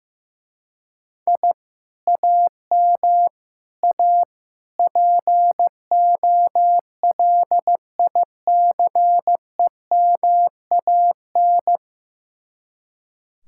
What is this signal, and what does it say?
Morse code, heard as a keyed tone, about 15 wpm, 700 Hz. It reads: I AM A POLICEMAN
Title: Speed-Racing Common QSO Phrases Encore - 15wpm